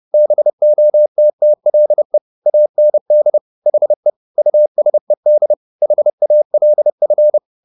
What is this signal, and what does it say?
Morse code, heard as a keyed tone, about 30 wpm, 600 Hz. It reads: BOTTLE AND HE USED HALF